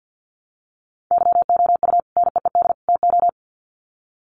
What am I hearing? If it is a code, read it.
Morse code, heard as a keyed tone, about 50 wpm, 700 Hz. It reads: YOU NEED TO
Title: Sets of 3 Words 50wpm